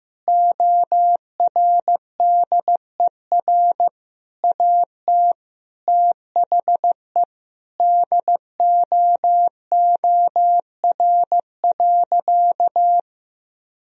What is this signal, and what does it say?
Morse code, heard as a keyed tone, about 15 wpm, 700 Hz. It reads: ORDER AT THE DOOR.
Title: Sentences from Top 1000 Words - Review Part 2 - 15wpm